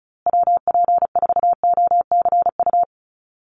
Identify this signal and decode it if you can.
Morse code, heard as a keyed tone, about 34 wpm, 700 Hz. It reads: WP4OCU
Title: USA Call Signs Repeated 3x 35wpm